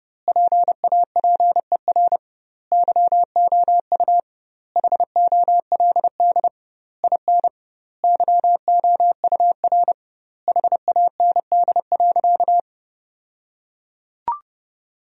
Morse code, heard as a keyed tone, about 32 wpm, 700 Hz. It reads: PAPER YOU HOLD IN YOUR HAND.
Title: 2x Word-Spacing - Sentences from Top 400 Words - 30wpm